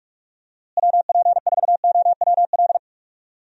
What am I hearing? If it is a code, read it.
Morse code, heard as a keyed tone, about 44 wpm, 700 Hz. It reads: WP4OWL
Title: USA Call Signs - 6 Characters - Repeated 3x - 45wpm